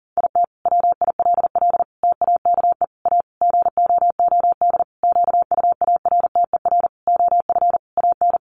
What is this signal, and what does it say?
Morse code, heard as a keyed tone, about 40 wpm, 700 Hz. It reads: IT WILL TAKE A GOOD QUARTER OF AN
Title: Sentences from Top 1300 Words 40wpm